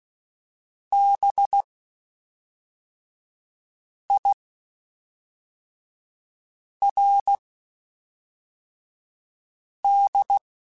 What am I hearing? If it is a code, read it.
Morse code, heard as a keyed tone, about 16 wpm, spaced as for 3 wpm, 800 Hz. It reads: BIRD